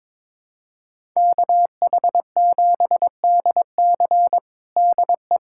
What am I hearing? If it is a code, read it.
Morse code, heard as a keyed tone, about 22 wpm, 700 Hz. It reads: KH7DC DE
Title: USA Callsign DE Callsign - 5 Characters - 22wpm